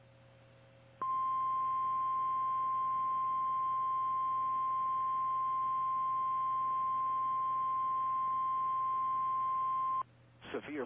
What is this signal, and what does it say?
Emergency Alert System (SAME) tones (the 1050 Hz attention tone), with speech. It is heard from 1.0 to 10.1 s.
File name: eas / KKPRKXPN / KKPRKXPN